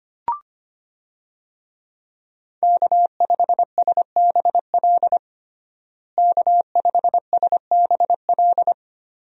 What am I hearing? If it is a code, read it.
Morse code, heard as a keyed tone, about 25 wpm, 700 Hz. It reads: K5SBL K5SBL
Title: USA Call Signs - 5 Characters - Repeated 2x - 25wpm